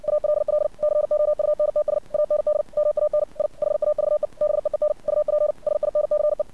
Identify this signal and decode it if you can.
Morse code, heard as a keyed tone, about 56 wpm, 600 Hz. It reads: ALL COUNTS AND MAN I HAVE BEEN UP SINCE
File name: archived / CRSnet